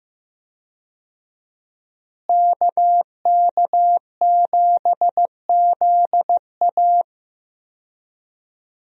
Morse code, heard as a keyed tone, about 15 wpm, 700 Hz. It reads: KK7ZA
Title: USA Call Signs - 5 Characters - Repeated 3x - 15wpm